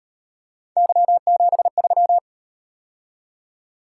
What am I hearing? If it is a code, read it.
Morse code, heard as a keyed tone, about 38 wpm, 700 Hz. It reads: Y73